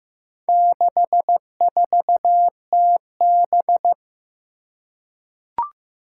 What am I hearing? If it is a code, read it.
Morse code, heard as a keyed tone, about 15 wpm, 700 Hz. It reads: 64TB
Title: Four Challenging Letter-Number Combinations 15wpm